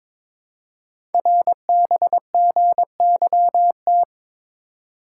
Morse code, heard as a keyed tone, about 22 wpm, 700 Hz. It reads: RBGYT
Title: Code Groups - 5 Letters - 22wpm